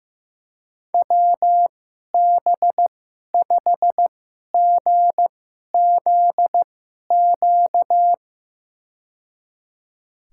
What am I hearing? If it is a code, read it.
Morse code, heard as a keyed tone, about 15 wpm, spaced as for 11 wpm, 700 Hz. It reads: WB5GZQ
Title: USA Call Signs - 6 Characters - 2x Character Spacing - 15wpm